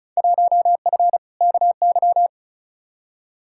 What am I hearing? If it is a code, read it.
Morse code, heard as a keyed tone, about 36 wpm, 690 Hz. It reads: 1F KY
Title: Field Day Exchanges Encore - 35wpm